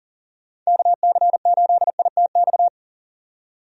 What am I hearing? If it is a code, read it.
Morse code, heard as a keyed tone, about 40 wpm, 700 Hz. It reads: KC8ITX
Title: USA Call Signs - 6 Characters - 40wpm